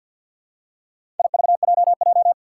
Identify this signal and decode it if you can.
Morse code, heard as a keyed tone, about 49 wpm, 700 Hz. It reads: IVJJ
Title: Four Challenging Letter-Number Combinations 50wpm